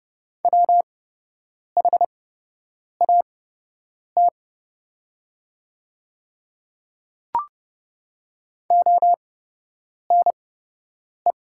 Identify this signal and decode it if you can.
Morse code, heard as a keyed tone, about 31 wpm, spaced as for 8 wpm, 700 Hz. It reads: WHAT ONE
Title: 8x Character Spacing - Top 100 Words - 30wpm